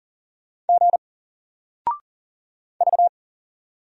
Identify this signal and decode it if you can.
Morse code, heard as a keyed tone, about 40 wpm, 700 Hz. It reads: G V
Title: Single Character (Letter:Number:Punctuation:Prosign) Alt 40wpm